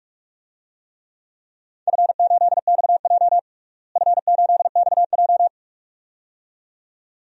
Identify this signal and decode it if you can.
Morse code, heard as a keyed tone, about 42 wpm, 700 Hz. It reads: F8XJ F8XJ
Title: DXCC-100 Call Signs - 4 Characters - Repeated 2x - 45wpm